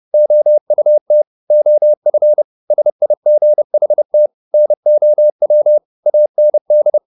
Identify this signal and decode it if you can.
Morse code, heard as a keyed tone, about 30 wpm, 600 Hz. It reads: OUT OF SIGHT NOW AND
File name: the old man and the sea / the old man and the sea 1 30wpm